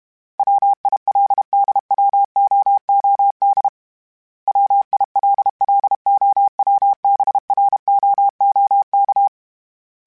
Morse code, heard as a keyed tone, about 32 wpm, 800 Hz. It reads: WILDWOOD WILLOWBROOK
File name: fromSMs / FromSMs mp3s / Cities@30